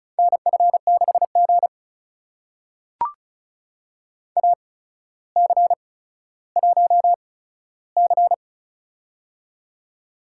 Morse code, heard as a keyed tone, about 34 wpm, spaced as for 15 wpm, 700 Hz. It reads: NF6G AC1C
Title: USA Call Signs - 4 Characters - 8x Character Spacing - 35wpm